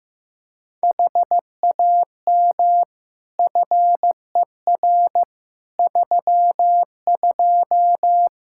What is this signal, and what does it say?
Morse code, heard as a keyed tone, about 15 wpm, 700 Hz. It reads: HAM FER 32
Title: Practice QSO - Exchange 3 - More Info - 15wpm